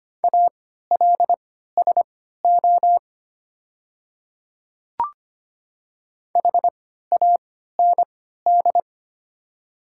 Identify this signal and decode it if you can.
Morse code, heard as a keyed tone, about 25 wpm, spaced as for 14 wpm, 700 Hz. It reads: ALSO HAND
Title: 3x Character Spacing - Top 300 Words - 25wpm